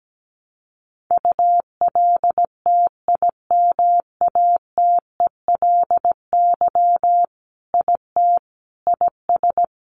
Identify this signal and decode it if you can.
Morse code, heard as a keyed tone, about 17 wpm, 700 Hz. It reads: ULTIMATELY IT IS
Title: Sentences from Top 1200 Words 17wpm